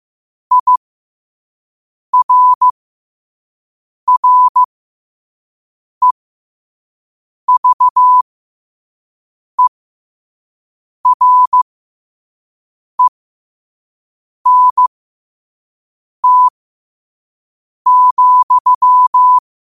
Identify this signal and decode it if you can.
Morse code, heard as a keyed tone, about 15 wpm, spaced as for 5 wpm, 1000 Hz. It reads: IRREVERENT,
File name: qotdfiles / 5wpm